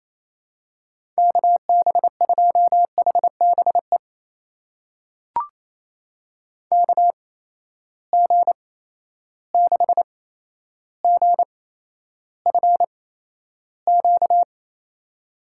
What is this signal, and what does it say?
Morse code, heard as a keyed tone, about 28 wpm, spaced as for 12 wpm, 700 Hz. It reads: KB2HBE KG6GFQ
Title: USA Call Signs - 6 Characters - 8x Character Spacing - 28wpm